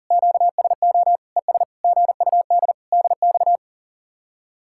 Morse code, heard as a keyed tone, about 50 wpm, 700 Hz. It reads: QSO ES GUD DX